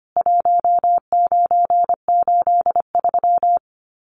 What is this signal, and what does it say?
Morse code, heard as a keyed tone, about 25 wpm, 700 Hz. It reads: 1983
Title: Common QSO Phrases 25wpm